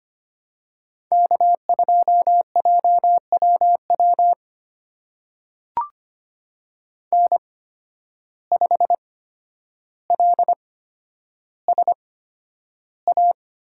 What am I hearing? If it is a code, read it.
Morse code, heard as a keyed tone, about 25 wpm, spaced as for 11 wpm, 700 Hz. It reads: K2JWW N5LSA
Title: USA Call Signs - 5 Characters - 8x Character Spacing - 25wpm